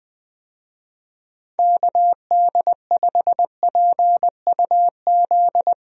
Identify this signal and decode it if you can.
Morse code, heard as a keyed tone, about 21 wpm, 700 Hz. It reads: KD5PUZ
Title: USA Call Signs - 6 Characters - Repeated 2x - 20wpm